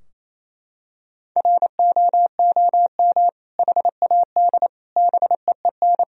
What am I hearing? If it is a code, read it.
Morse code, heard as a keyed tone, about 28 wpm, 700 Hz. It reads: ROOM HAD BEEN
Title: Sentences from Top 1000 Words - Review Part 3 - 28wpm